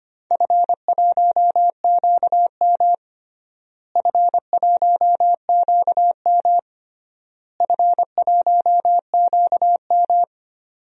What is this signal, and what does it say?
Morse code, heard as a keyed tone, about 25 wpm, 700 Hz. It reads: F1QM F1QM F1QM
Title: DXCC-100 Call Signs - 4 Characters - Repeated 3x - 25wpm